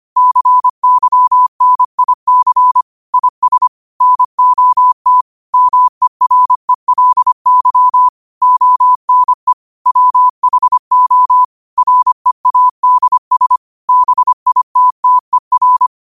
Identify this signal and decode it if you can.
Morse code, heard as a keyed tone, about 25 wpm, 1000 Hz. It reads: CYNIC IS NOT MERELY ONE WHO READS BITTER
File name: qotdfiles / 25wpm